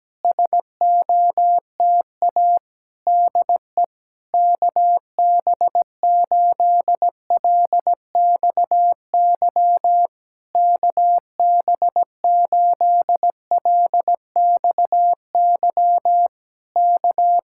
Morse code, heard as a keyed tone, about 17 wpm, 700 Hz. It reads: SOTA DE KB8LXY KB8LXY K
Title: Speed-Racing Common QSO Phrases - 17wpm